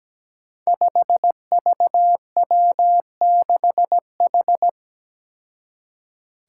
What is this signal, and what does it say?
Morse code, heard as a keyed tone, about 17 wpm, 700 Hz. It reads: 5VW6H
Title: Five Challenging Letter-Number Combinations 17wpm